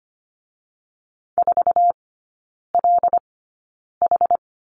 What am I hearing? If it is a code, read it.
Morse code, heard as a keyed tone, about 25 wpm, spaced as for 9 wpm, 700 Hz. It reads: 4LH
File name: Lesson 17e - Letter u - Three Characters 25wpm